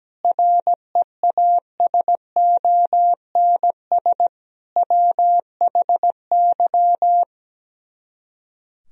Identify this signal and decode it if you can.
Morse code, heard as a keyed tone, about 17 wpm, 700 Hz. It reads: REASONS WHY
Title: Speed-Racing - Sets of 5 Words 17wpm